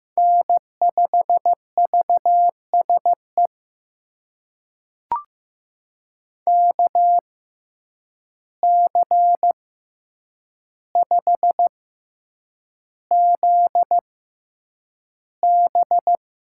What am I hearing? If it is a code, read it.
Morse code, heard as a keyed tone, about 15 wpm, spaced as for 8 wpm, 700 Hz. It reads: N5VSE KC5ZB
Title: USA Call Signs - 5 Characters - 6x Character Spacing - 15wpm